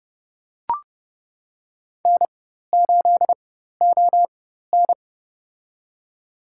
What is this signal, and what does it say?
Morse code, heard as a keyed tone, about 30 wpm, spaced as for 14 wpm, 700 Hz. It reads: N8ON
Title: USA Call Signs - 4 Characters - 4x Character Spacing - 30wpm